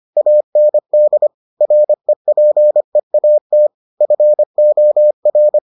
Morse code, heard as a keyed tone, about 25 wpm, 600 Hz. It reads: AND REPEAT FOR